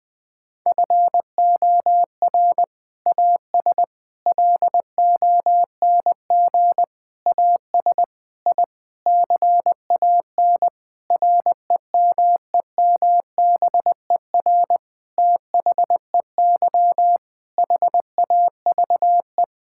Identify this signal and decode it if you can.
Morse code, heard as a keyed tone, about 20 wpm, 700 Hz. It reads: FOR AS LONG AS I CAN REMEMBER THEY HAVE
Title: Sentences with 7-Word Phrases - 20wpm